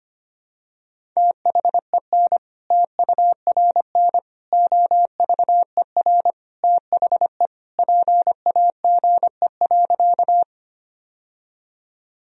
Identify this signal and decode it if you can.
Morse code, heard as a keyed tone, about 25 wpm, 700 Hz. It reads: THEN TURN OVER THE PAGE.
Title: Sentences from Top 1000 Words - Review Part 1 - 25wpm